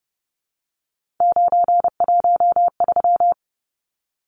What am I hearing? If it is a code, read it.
Morse code, heard as a keyed tone, about 30 wpm, 700 Hz. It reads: 913
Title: Numbers - 3-Digit 30wpm